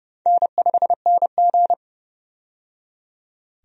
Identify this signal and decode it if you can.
Morse code, heard as a keyed tone, about 30 wpm, 700 Hz. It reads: N5NG
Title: USA Call Signs - 4 Characters - 30wpm